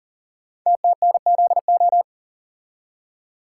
Morse code, heard as a keyed tone, about 38 wpm, 690 Hz. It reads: TTNZO
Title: Code Groups - 5 Letters - 40wpm